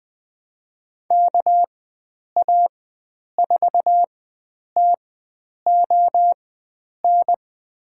Morse code, heard as a keyed tone, about 21 wpm, spaced as for 9 wpm, 700 Hz. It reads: KA4TON
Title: USA Call Signs - 6 Characters - 4x Character Spacing - 20wpm